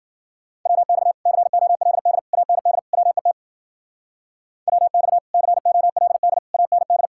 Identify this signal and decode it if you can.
Morse code, heard as a keyed tone, about 50 wpm, 700 Hz. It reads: WX COLD AND WET WX COLD AND
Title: Speed-Racing Common QSO Phrases - 40wpm